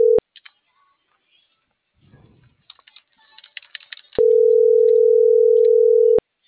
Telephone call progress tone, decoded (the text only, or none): ringback tone